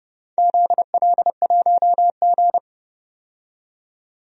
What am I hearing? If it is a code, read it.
Morse code, heard as a keyed tone, about 32 wpm, 700 Hz. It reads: ZL1G
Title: DXCC-100 Call Signs - 4 Characters - 30wpm